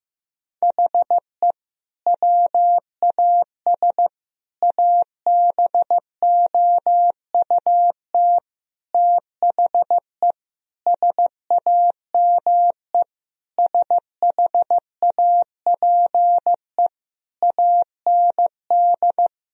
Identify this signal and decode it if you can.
Morse code, heard as a keyed tone, about 15 wpm, 700 Hz. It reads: HE WAS ABOUT THE SAME SHAPE AND
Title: Sentences from Top 1000 Words - Review Part 3 - 15wpm